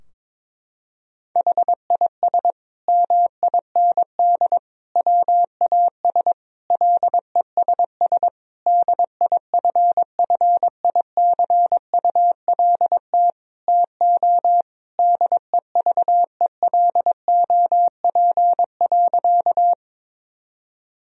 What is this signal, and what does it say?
Morse code, heard as a keyed tone, about 22 wpm, 700 Hz. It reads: HIS MIND WAS LESS DIFFICULT TO DEVELOP.
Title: Sentences from Top 1000 Words - Review Part 2 - 22wpm